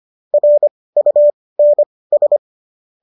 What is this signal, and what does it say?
Morse code, heard as a keyed tone, about 25 wpm, spaced as for 18 wpm, 590 Hz. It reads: RUNS